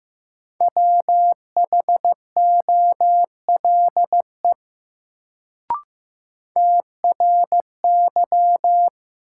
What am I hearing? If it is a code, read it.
Morse code, heard as a keyed tone, about 16 wpm, 700 Hz. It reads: WHOLE TRY
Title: Sets of 2 Words - New Word Drill - 15wpm